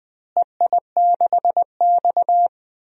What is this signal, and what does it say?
Morse code, heard as a keyed tone, about 20 wpm, 700 Hz. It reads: EI6X
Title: DXCC-100 Call Signs - 4 Characters - Repeated 3x - 20wpm